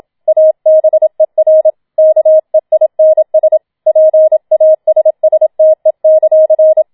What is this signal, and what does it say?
Morse code, heard as a keyed tone, about 27 wpm, 630 Hz. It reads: ABER KEINS PASSTE;